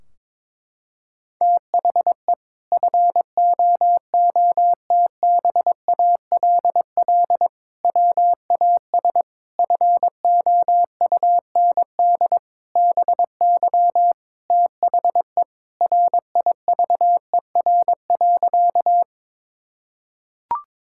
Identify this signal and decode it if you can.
Morse code, heard as a keyed tone, about 22 wpm, 700 Hz. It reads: THE FOOTBALL WAS FOUND BY THE RIVER.
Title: Sentences from Top 1600 Words 22wpm